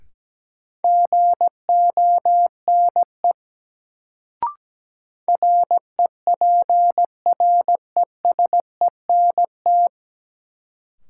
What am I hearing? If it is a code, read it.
Morse code, heard as a keyed tone, about 17 wpm, 700 Hz. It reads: GONE REPRESENT
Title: Sentences from Top 1200 Words - New Word Drill Alt - 17wpm